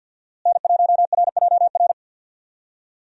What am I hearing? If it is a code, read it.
Morse code, heard as a keyed tone, about 52 wpm, 700 Hz. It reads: N1RJR
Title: USA Call Signs - 5 Characters - Repeated 3x - 50wpm